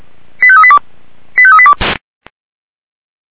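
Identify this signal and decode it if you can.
Five-tone selective call (ZVEI), heard as quick runs of five tones, tones of about 70 ms. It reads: 85382 85382
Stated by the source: ZVEI five-tone selective call 89262 received on SDR.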